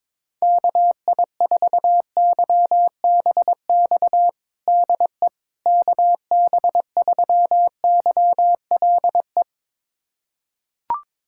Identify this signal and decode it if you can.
Morse code, heard as a keyed tone, about 22 wpm, 700 Hz. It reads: KI4YBX DE KB3YLE
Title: USA Callsign DE Callsign - 6 Characters - 22wpm